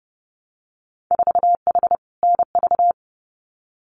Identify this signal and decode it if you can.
Morse code, heard as a keyed tone, about 30 wpm, 700 Hz. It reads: 4H NV
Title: Winter Field Day Exchanges Encore 30wpm